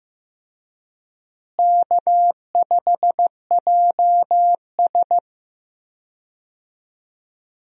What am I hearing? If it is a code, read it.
Morse code, heard as a keyed tone, about 14 wpm, 700 Hz. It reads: K5JS
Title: USA Call Signs - 4 Characters - Repeated 3x - 15wpm